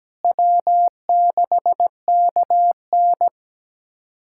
Morse code, heard as a keyed tone, about 17 wpm, 700 Hz. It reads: W6KN
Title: SOTA QSO Practice 17wpm